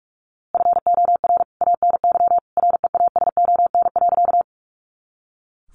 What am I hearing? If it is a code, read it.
Morse code, heard as a keyed tone, about 44 wpm, 700 Hz. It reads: FOR ANY REASON.
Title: A Sound Of Thunder 45wpm